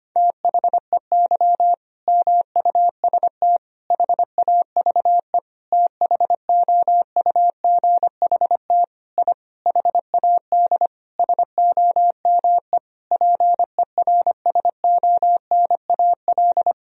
Morse code, heard as a keyed tone, about 25 wpm, 700 Hz. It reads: THEY MUST HAVE THOUGHT I HAD SOME PERSONAL
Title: Sentences from Top 1000 Words - Review Part 4 - 25wpm